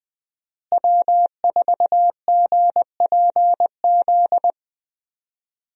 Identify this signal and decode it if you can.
Morse code, heard as a keyed tone, about 20 wpm, 700 Hz. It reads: W4GPZ
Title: USA Call Signs - 5 Characters - Repeated 3x - 20wpm